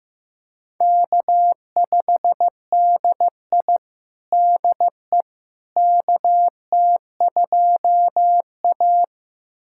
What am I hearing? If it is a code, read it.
Morse code, heard as a keyed tone, about 15 wpm, 700 Hz. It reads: K5DI DE KT2A
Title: USA Callsign DE Callsign - 4 Characters - 15wpm